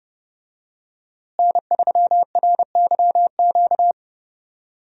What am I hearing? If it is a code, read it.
Morse code, heard as a keyed tone, about 30 wpm, 700 Hz. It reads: N3RYQ
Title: USA Call Signs - 5 Characters - Repeated 2x - 30wpm